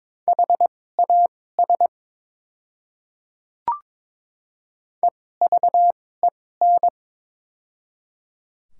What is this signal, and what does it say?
Morse code, heard as a keyed tone, about 22 wpm, spaced as for 16 wpm, 700 Hz. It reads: HAS EVEN
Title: 2x Character Spacing - Top 200 Words - 22wpm